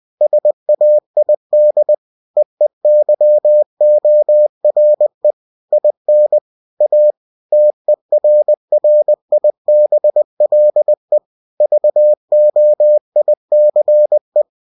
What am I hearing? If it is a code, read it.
Morse code, heard as a keyed tone, about 20 wpm, 600 Hz. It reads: SAID EEYORE IN A TERRIBLE VOICE